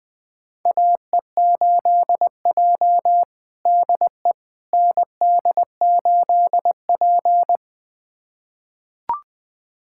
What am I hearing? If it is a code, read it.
Morse code, heard as a keyed tone, about 20 wpm, 700 Hz. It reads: AE8J DE ND8P
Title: USA Callsign DE Callsign - 4 Characters - 20wpm